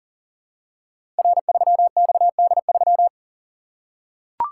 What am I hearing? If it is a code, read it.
Morse code, heard as a keyed tone, about 40 wpm, 700 Hz. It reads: R3XD3
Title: Code Groups - 5 Letter:Numbers - 40wpm